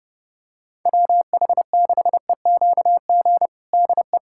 Morse code, heard as a keyed tone, about 30 wpm, 700 Hz. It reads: WH6EQG DE
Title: USA Callsign DE Callsign - 6 Characters - 30wpm